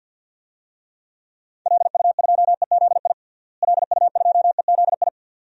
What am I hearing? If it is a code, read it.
Morse code, heard as a keyed tone, about 51 wpm, 700 Hz. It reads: LU2EZI LU2EZI
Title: DXCC-100 Call Signs - 6 Characters - Repeated 2x - 50wpm